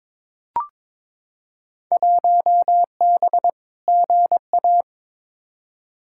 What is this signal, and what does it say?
Morse code, heard as a keyed tone, about 21 wpm, 700 Hz. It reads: 1B GA
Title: Field Day Exchanges Encore - 22wpm